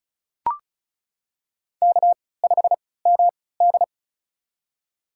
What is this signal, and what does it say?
Morse code, heard as a keyed tone, about 36 wpm, spaced as for 20 wpm, 700 Hz. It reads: K5MD
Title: USA Call Signs - 4 Characters - 3x Character Spacing - 35wpm